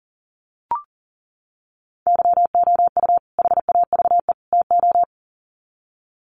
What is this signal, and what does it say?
Morse code, heard as a keyed tone, about 40 wpm, 700 Hz. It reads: YOU HAVE TO
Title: Sets of 3 Words 40wpm